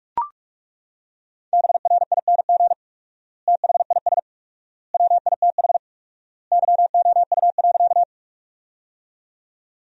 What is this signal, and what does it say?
Morse code, heard as a keyed tone, about 42 wpm, 690 Hz. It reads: BRING THIS WITH YOU.
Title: 4x Word-Spacing - Sentences from Top 500 Words - 45wpm